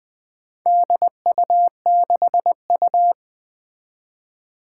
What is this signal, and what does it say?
Morse code, heard as a keyed tone, about 20 wpm, 700 Hz. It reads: DU6U
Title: Four Challenging Letter-Number Combinations 20wpm